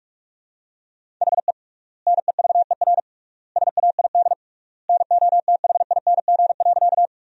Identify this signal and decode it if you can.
Morse code, heard as a keyed tone, about 46 wpm, 700 Hz. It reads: HE NEVER SAID NOTHING.
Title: 3x Word-Spacing - Sentences from Top 500 Words - 45wpm